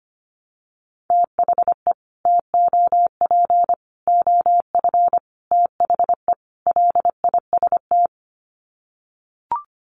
Morse code, heard as a keyed tone, about 25 wpm, 700 Hz. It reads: THE TOP OF THE LIST
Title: Sets of 5 Words Encore2 25wpm